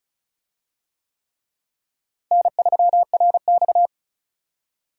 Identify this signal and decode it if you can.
Morse code, heard as a keyed tone, about 36 wpm, 700 Hz. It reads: N3RX
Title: USA Call Signs - 4 Characters - Repeated 3x - 35wpm